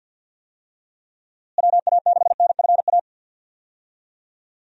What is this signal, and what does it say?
Morse code, heard as a keyed tone, about 48 wpm, 700 Hz. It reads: WA6NFA
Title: USA Call Signs - 6 Characters - Repeated 2x - 50wpm